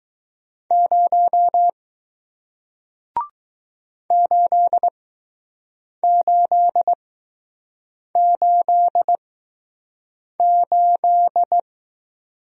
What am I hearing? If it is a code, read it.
Morse code, heard as a keyed tone, about 20 wpm, spaced as for 6 wpm, 700 Hz. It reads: 0 8888